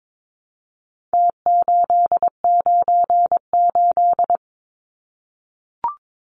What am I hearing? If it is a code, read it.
Morse code, heard as a keyed tone, about 22 wpm, 700 Hz. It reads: T898
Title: Numbers - Contest Style - 4-Digit 22wpm